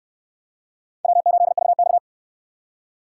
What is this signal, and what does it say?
Morse code, heard as a keyed tone, about 68 wpm, 700 Hz. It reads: W8VX